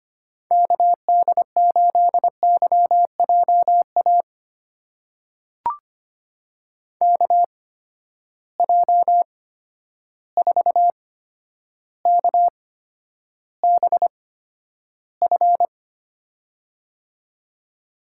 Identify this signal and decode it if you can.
Morse code, heard as a keyed tone, about 25 wpm, spaced as for 11 wpm, 700 Hz. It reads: KD8YJA KJ4KBF
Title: USA Call Signs - 8x Character Spacing - 25wpm